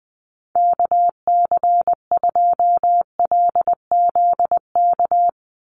Morse code, heard as a keyed tone, about 20 wpm, 700 Hz. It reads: KC2LZK
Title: USA Call Signs 20wpm